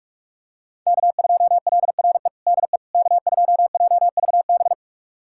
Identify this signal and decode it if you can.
Morse code, heard as a keyed tone, about 46 wpm, 700 Hz. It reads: K2LRE DE K2JVB
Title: USA Callsign DE Callsign - 5 Characters - 45wpm